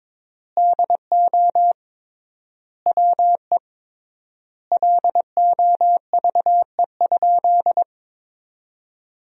Morse code, heard as a keyed tone, about 22 wpm, 700 Hz. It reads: DO WE LOVE?
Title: 3x Word-Spacing - Sentences from Top 400 Words - 22wpm